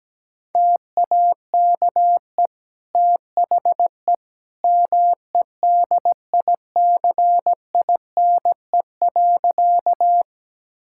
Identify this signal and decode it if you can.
Morse code, heard as a keyed tone, about 17 wpm, 700 Hz. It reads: TAKE THE MEDICINE.
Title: Sentences from Top 1000 Words - Review Part 6 - 17wpm